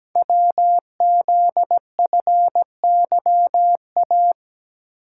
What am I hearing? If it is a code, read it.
Morse code, heard as a keyed tone, about 17 wpm, 700 Hz. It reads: WZFYA
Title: Code Groups - 5 Letters - 17wpm